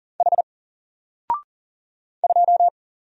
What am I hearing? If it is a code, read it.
Morse code, heard as a keyed tone, about 40 wpm, 700 Hz. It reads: H 2
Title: Single Character (Letter:Number:Punctuation:Prosign) Alt 40wpm